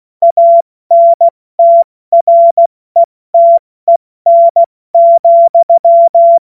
Morse code, heard as a keyed tone, about 16 wpm, 680 Hz. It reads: ANTRETEN,
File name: Morsekurs/Witze